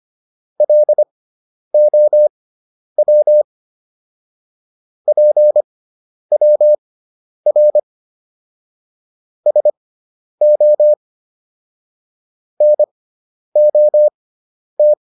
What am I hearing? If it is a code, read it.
Morse code, heard as a keyed tone, about 25 wpm, spaced as for 10 wpm, 600 Hz. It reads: LOW PWR SO NOT